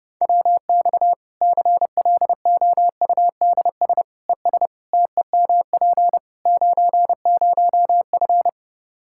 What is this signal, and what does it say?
Morse code, heard as a keyed tone, about 32 wpm, 700 Hz. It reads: WX CLOUDS ES TEMP 90F
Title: Practice QSO - Exchange 2 - Info - 30wpm